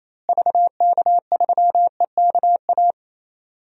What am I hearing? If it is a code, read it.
Morse code, heard as a keyed tone, about 28 wpm, 700 Hz. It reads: VK3EKA
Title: DXCC-100 Call Signs - 6 Characters - Repeated 2x - 28wpm